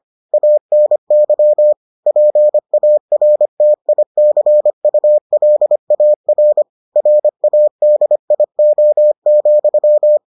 Morse code, heard as a keyed tone, about 25 wpm, 600 Hz. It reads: ANY PARTICULAR RADIO,